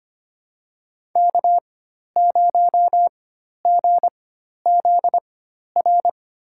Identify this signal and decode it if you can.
Morse code, heard as a keyed tone, about 25 wpm, spaced as for 12 wpm, 700 Hz. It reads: K0GZR
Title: USA Call Signs - 4x Character Spacing - 25wpm